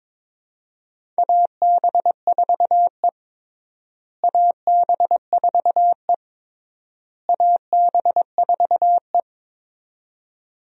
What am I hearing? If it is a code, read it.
Morse code, heard as a keyed tone, about 22 wpm, 700 Hz. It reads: AB4E AB4E AB4E
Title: USA Call Signs - 4 Characters - Repeated 3x - 22wpm